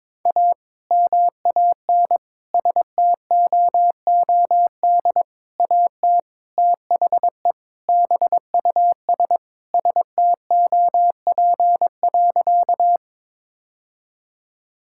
Morse code, heard as a keyed tone, about 22 wpm, 700 Hz. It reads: A MAN STOOD AT THE BUS STOP.
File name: practice / 22wpm / Sentences with 2-Word Phrases - 22wpm